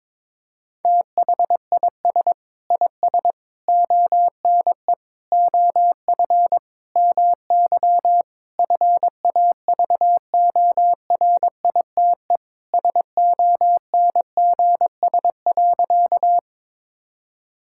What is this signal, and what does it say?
Morse code, heard as a keyed tone, about 22 wpm, 700 Hz. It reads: THIS IS ONE OF MY FAVORITE SONGS.
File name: practice / 22wpm / Sentences with 4-Word Phrases - 22wpm